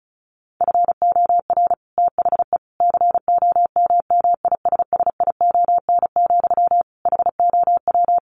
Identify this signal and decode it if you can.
Morse code, heard as a keyed tone, about 34 wpm, 700 Hz. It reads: FOR THE COMMISSION, HOW
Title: Sentences from Top 1400 Words 35wpm